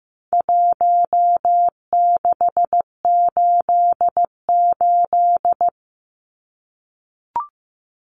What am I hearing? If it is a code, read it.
Morse code, heard as a keyed tone, about 15 wpm, 700 Hz. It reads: 1688
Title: Numbers - Contest Style - 4-Digit 15wpm